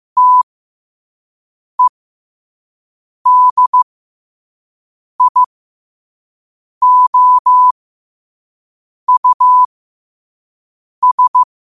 Morse code, heard as a keyed tone, about 15 wpm, spaced as for 5 wpm, 1000 Hz. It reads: TEDIOUS
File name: qotdfiles / 5wpm